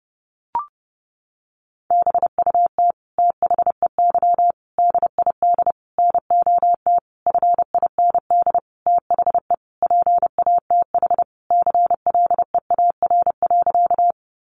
Morse code, heard as a keyed tone, about 30 wpm, 700 Hz. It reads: BUT THEY DID NOT FIND THE PATH CLEAR.
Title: Sentences from Top 1400 Words 30wpm